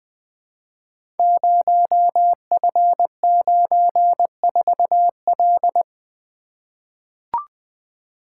Morse code, heard as a keyed tone, about 20 wpm, 700 Hz. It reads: 0F94L
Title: Code Groups - 5 Letter:Numbers - 20wpm